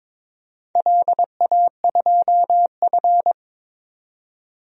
Morse code, heard as a keyed tone, about 22 wpm, 700 Hz. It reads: LA2F
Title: DXCC-100 Call Signs - 4 Characters - 22wpm